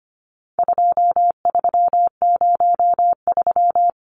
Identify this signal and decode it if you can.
Morse code, heard as a keyed tone, about 25 wpm, 700 Hz. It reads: 2303
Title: Numbers - 4-Digit 25wpm